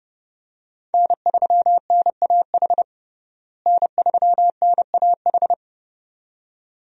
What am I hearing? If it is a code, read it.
Morse code, heard as a keyed tone, about 30 wpm, 700 Hz. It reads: N3NAH N3NAH
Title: USA Call Signs - 5 Characters - Repeated 2x - 30wpm